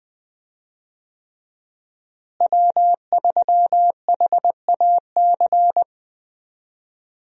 Morse code, heard as a keyed tone, about 20 wpm, 700 Hz. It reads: W3HAC
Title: USA Call Signs - 5 Characters - Repeated 3x - 20wpm